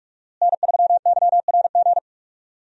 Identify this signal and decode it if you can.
Morse code, heard as a keyed tone, about 46 wpm, 700 Hz. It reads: N3YRG
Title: USA Call Signs - 5 Characters - Repeated 3x - 45wpm